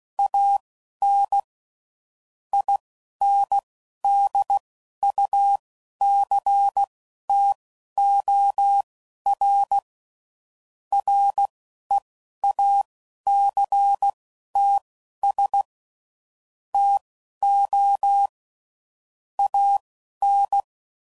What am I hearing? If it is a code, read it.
Morse code, heard as a keyed tone, about 16 wpm, spaced as for 11 wpm, 800 Hz. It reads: AN INDUCTOR REACTS TO AN
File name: CW Code Groups @10WPM / Technical Subjects / XL